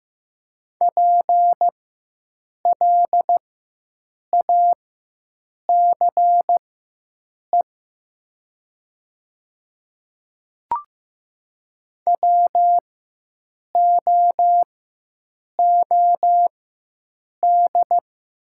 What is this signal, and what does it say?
Morse code, heard as a keyed tone, about 15 wpm, spaced as for 7 wpm, 700 Hz. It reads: PLACE WOOD
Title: 4x Character Spacing - Top 300 Words - 15wpm